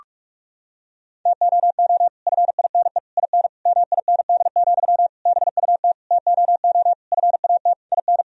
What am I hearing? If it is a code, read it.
Morse code, heard as a keyed tone, about 46 wpm, 700 Hz. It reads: TOO FINE IN MIND, BUT TOO FAT IN
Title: Sentences from Top 1000 Words - Review Part 4 - 45wpm